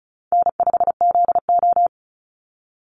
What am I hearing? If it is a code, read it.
Morse code, heard as a keyed tone, about 34 wpm, 700 Hz. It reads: N5ZO
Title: Easy Call Signs Repeated 2x - 35wpm